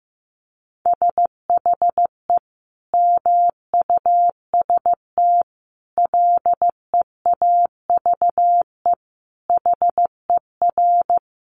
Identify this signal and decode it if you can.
Morse code, heard as a keyed tone, about 15 wpm, 700 Hz. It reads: SHE MUST LEAVE HER
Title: Sentences from Top 1300 Words 15wpm